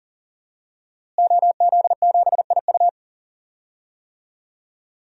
Morse code, heard as a keyed tone, about 37 wpm, 700 Hz. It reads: OZ7IU
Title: DXCC-100 Call Signs - 5 Characters - Repeated 2x - 40wpm